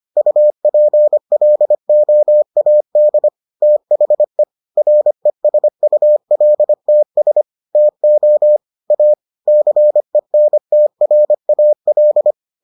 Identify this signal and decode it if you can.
Morse code, heard as a keyed tone, about 25 wpm, 600 Hz. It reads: UPLOAD THE RESULTS TO A CENTRAL